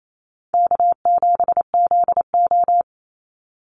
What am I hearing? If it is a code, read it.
Morse code, heard as a keyed tone, about 28 wpm, 700 Hz. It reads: K7ZO
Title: NA SOTA Chaser Callsigns 28wpm